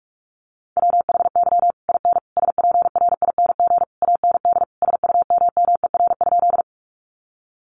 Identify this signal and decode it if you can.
Morse code, heard as a keyed tone, about 44 wpm, 700 Hz. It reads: WHY IN SPRING AND SUMMER?
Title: Sentences from Top 1200 Words 45wpm